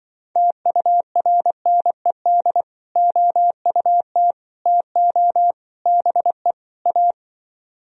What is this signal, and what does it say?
Morse code, heard as a keyed tone, about 25 wpm, 700 Hz. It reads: TURNED OUT TO BE A
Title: Speed-Racing - Sets of 5 Words 20wpm